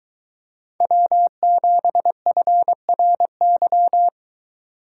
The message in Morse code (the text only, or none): W7FRY